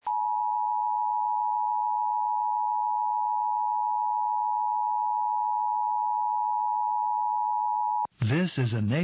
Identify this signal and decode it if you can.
Emergency Alert System (SAME) tones (the 853 and 960 Hz attention tone), with speech. It is heard from 0.0 to 8.1 s.
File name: eas / KKPRKXPN / KKPRKXPN